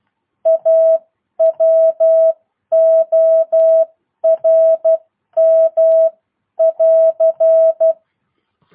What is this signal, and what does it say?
Morse code, heard as a keyed tone, about 12 wpm, spaced as for 10 wpm, 650 Hz. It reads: A WORM <AR>